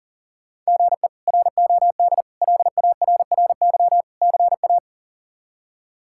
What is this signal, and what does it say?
Morse code, heard as a keyed tone, about 40 wpm, 700 Hz. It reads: GE ROD LARRY CA
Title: K1USN SST QSO Practice - 40wpm